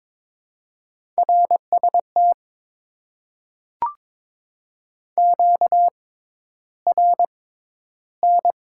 Morse code, heard as a keyed tone, about 22 wpm, spaced as for 11 wpm, 700 Hz. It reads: RST QRN
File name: Top 90 QSO Elements - 6x Character Spacing - 22wpm